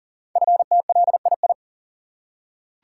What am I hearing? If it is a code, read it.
Morse code, heard as a keyed tone, about 40 wpm, 700 Hz. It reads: FTLII
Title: Code Groups - 5 Letters - 40wpm